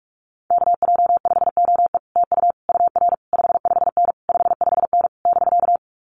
Morse code, heard as a keyed tone, about 46 wpm, 700 Hz. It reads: KJ5OE TU UR 55N 55N <BK>
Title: Common QSO Phrases Encore 45wpm